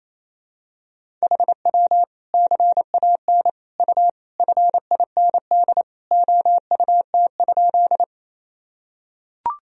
Morse code, heard as a keyed tone, about 28 wpm, 700 Hz. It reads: HW CAN U FIND OUT?
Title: Sentences from Top 200 Words and Abbreviations 28wpm